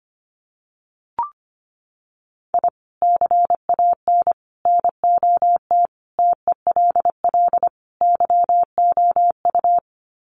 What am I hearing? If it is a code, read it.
Morse code, heard as a keyed tone, about 25 wpm, 700 Hz. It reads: I CAN NOT TELL YOU
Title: Sets of 5 Words 25wpm